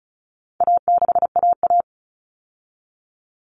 Morse code, heard as a keyed tone, about 34 wpm, 700 Hz. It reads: A6AA
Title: All Cut Numbers - 4-Digit 35wpm